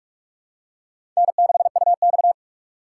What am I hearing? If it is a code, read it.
Morse code, heard as a keyed tone, about 46 wpm, 700 Hz. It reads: N6UX
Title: USA Call Signs - 4 Characters - 6x Character Spacing - 45wpm